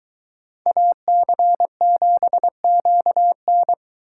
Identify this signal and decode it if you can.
Morse code, heard as a keyed tone, about 22 wpm, 700 Hz. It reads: AC7QN